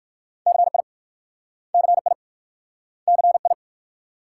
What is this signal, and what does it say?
Morse code, heard as a keyed tone, about 56 wpm, 700 Hz. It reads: CI CI CI
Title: Speed-Racing Multi-letter Phonemes - 45wpm